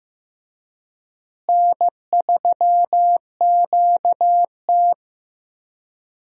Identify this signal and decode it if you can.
Morse code, heard as a keyed tone, about 15 wpm, 700 Hz. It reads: N3QT
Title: USA Call Signs - 4 Characters - Repeated 3x - 15wpm